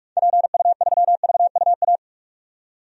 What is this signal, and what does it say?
Morse code, heard as a keyed tone, about 42 wpm, 700 Hz. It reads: PU3VUA
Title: DXCC-100 Call Signs - 6 Characters - 45wpm